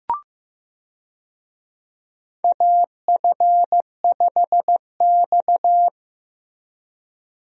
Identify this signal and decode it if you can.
Morse code, heard as a keyed tone, about 16 wpm, 700 Hz. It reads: AF5X
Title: USA Call Signs - 4 Characters - Repeated 3x - 15wpm